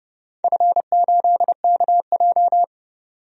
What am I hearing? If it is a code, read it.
Morse code, heard as a keyed tone, about 30 wpm, 700 Hz. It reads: F8KJ
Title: DXCC-100 Call Signs - 4 Characters - 30wpm